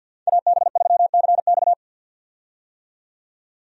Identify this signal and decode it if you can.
Morse code, heard as a keyed tone, about 51 wpm, 700 Hz. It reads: AB3CX
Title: USA Call Signs - 5 Characters - 50wpm